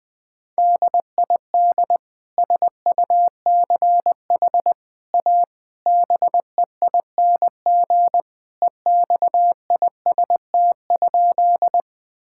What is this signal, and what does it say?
Morse code, heard as a keyed tone, about 21 wpm, 700 Hz. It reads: DID SUCH A BEING EXIST?
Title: Sentences from Top 1000 Words - Review Part 5 - 20wpm